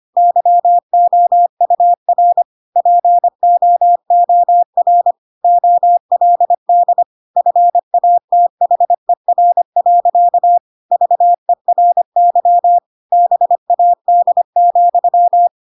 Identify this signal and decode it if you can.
Morse code, heard as a keyed tone, about 25 wpm, 700 Hz. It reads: YOUR POOR OLD FATHER. VERY BAD,
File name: Pinocchio / Pinocchio ch3 25 25 ik1hge